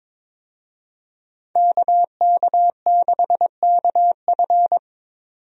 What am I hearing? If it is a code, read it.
Morse code, heard as a keyed tone, about 22 wpm, 700 Hz. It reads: KK6KF
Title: USA Call Signs - 5 Characters - 22wpm